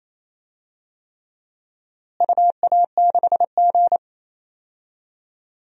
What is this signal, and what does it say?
Morse code, heard as a keyed tone, about 27 wpm, 700 Hz. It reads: UA6G
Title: DXCC-100 Call Signs - 4 Characters - Repeated 3x - 28wpm